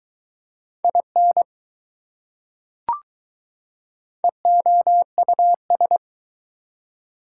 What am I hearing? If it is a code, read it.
Morse code, heard as a keyed tone, about 20 wpm, 700 Hz. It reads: IN EOUS